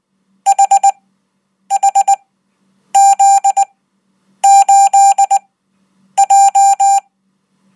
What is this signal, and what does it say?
Morse code, heard as a keyed tone, about 20 wpm, spaced as for 9 wpm, 790 Hz. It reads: HHZ8J
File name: Code